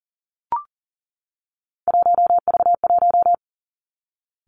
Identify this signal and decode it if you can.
Morse code, heard as a keyed tone, about 38 wpm, 700 Hz. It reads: 1V1
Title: Three Challenging Letter-Number Combinations 40wpm